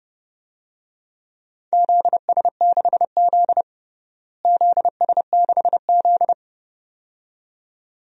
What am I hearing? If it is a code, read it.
Morse code, heard as a keyed tone, about 30 wpm, 700 Hz. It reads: ZS6Z ZS6Z
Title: DXCC-100 Call Signs - 4 Characters - Repeated 2x - 30wpm